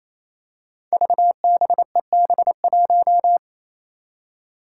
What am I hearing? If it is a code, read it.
Morse code, heard as a keyed tone, about 28 wpm, 700 Hz. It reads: VBEB1
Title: Five Challenging Letter-Number Combinations 28wpm